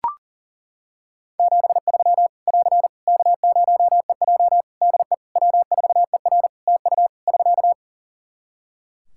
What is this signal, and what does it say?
Morse code, heard as a keyed tone, about 37 wpm, 700 Hz. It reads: 73 <AR> K0EJ DE W4ER TU <SK>
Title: Practice QSO - Exchange 4 - Ending - 40wpm